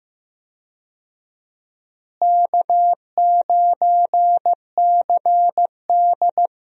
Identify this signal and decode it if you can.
Morse code, heard as a keyed tone, about 15 wpm, 700 Hz. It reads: K9CD
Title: USA Call Signs - 4 Characters - Repeated 2x - 15wpm